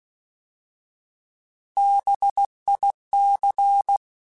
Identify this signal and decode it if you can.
Morse code, heard as a keyed tone, about 16 wpm, 800 Hz. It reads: BIC
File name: Alphabetical / Alphabetical mp3s / Bs @15